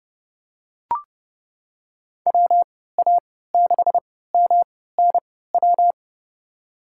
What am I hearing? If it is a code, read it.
Morse code, heard as a keyed tone, about 30 wpm, spaced as for 17 wpm, 700 Hz. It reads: WA6MNW
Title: USA Call Signs - 3x Character Spacing - 30wpm